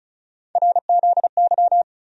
Morse code, heard as a keyed tone, about 36 wpm, 700 Hz. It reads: RZY
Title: USA Call Sign Postfixes - 3 Letters 35wpm